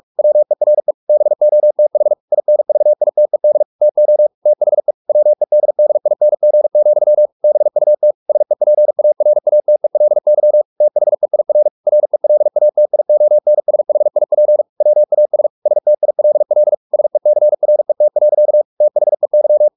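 Morse code, heard as a keyed tone, about 44 wpm, 600 Hz. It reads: WERE BOTH INVITED TO THE WEDDING, BUT SEPARATELY THEIR RELATIONSHIP WAS STILL SECRET. THEY
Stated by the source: Morse practice at 45 wpm, the novel Never by Ken Follett in English, from CW med Gnister.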